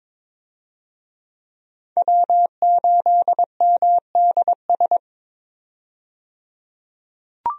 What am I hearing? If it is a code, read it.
Morse code, heard as a keyed tone, about 22 wpm, 700 Hz. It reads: W8MDS
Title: USA Call Signs - 5 Characters - 22wpm